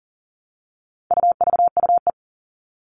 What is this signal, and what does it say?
Morse code, heard as a keyed tone, about 40 wpm, 700 Hz. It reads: UVUE
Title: All Cut Numbers - 4-Digit 40wpm